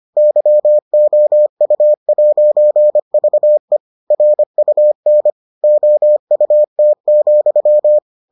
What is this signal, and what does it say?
Morse code, heard as a keyed tone, about 25 wpm, 600 Hz. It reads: YOU'VE RUN OUT,